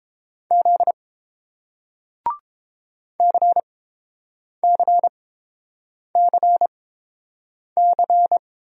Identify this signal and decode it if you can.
Morse code, heard as a keyed tone, about 30 wpm, spaced as for 7 wpm, 700 Hz. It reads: Z CCCC